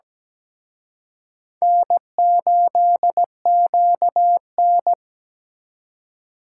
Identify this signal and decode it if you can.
Morse code, heard as a keyed tone, about 20 wpm, 700 Hz. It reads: N8QN
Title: Speed Racing - USA Call Signs 17wpm